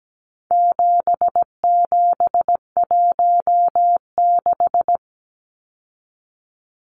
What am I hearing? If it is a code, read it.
Morse code, heard as a keyed tone, about 17 wpm, 700 Hz. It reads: 7716
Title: Numbers - 4-Digit 17wpm